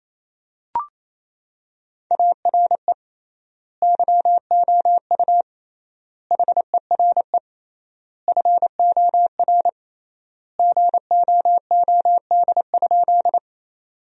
Morse code, heard as a keyed tone, about 27 wpm, 700 Hz. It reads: ARE YOU HERE FOR GOOD?
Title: 3x Word-Spacing - Sentences from Top 200 Words - 28wpm